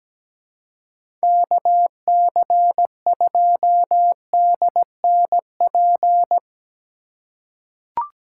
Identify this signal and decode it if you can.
Morse code, heard as a keyed tone, about 17 wpm, 700 Hz. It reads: KC2DNP
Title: USA Call Signs - 6 Characters - 2x Character Spacing - 17wpm